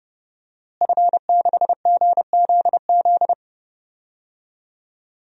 Morse code, heard as a keyed tone, about 30 wpm, 700 Hz. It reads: F6GZZ
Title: DXCC-100 Call Signs - 5 Characters - Repeated 3x - 30wpm